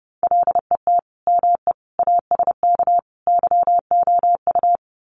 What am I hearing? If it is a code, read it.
Morse code, heard as a keyed tone, about 29 wpm, 700 Hz. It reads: LET ME ASK YOU
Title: Sets of 4 Words 30wpm